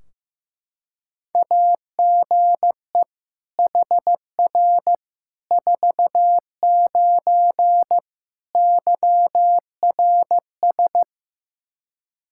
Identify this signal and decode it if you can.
Morse code, heard as a keyed tone, about 15 wpm, 700 Hz. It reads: AGE HR 49 YRS
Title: K1USN SST QSO Practice 15wpm